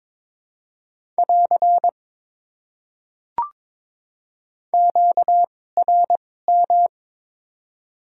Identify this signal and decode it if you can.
Morse code, heard as a keyed tone, about 21 wpm, spaced as for 16 wpm, 700 Hz. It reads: <AR> QRM
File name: Top 90 QSO Elements - 2x Character Spacing - 22wpm